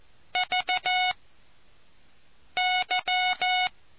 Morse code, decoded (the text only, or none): VY